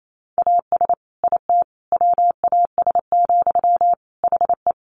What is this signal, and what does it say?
Morse code, heard as a keyed tone, about 28 wpm, 700 Hz. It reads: AS IT WAS, HE
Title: Sentences from Top 1400 Words 28wpm